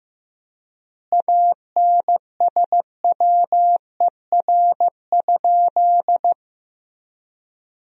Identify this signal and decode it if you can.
Morse code, heard as a keyed tone, about 15 wpm, 700 Hz. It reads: ANSWER?
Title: 3x Word-Spacing - Sentences from Top 300 Words - 15wpm